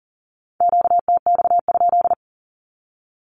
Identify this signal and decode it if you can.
Morse code, heard as a keyed tone, about 41 wpm, 700 Hz. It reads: QTX?
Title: Q-Codes with Spoken Meaning - 40wpm